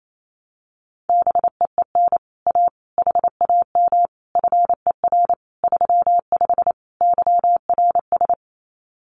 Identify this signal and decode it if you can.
Morse code, heard as a keyed tone, about 28 wpm, 700 Hz. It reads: BEEN A HAM FER 35 YRS
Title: Common QSO Phrases Encore 28wpm